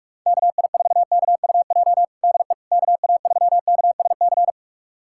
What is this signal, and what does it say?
Morse code, heard as a keyed tone, about 47 wpm, 700 Hz. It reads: KI4KUJ DE KA3KSC
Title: USA Callsign DE Callsign 45wpm